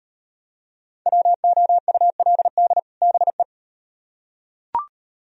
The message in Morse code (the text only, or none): WOULD BE